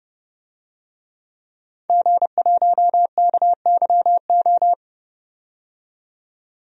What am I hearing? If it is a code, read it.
Morse code, heard as a keyed tone, about 31 wpm, 700 Hz. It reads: G1KYO
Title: DXCC-100 Call Signs - 5 Characters - 30wpm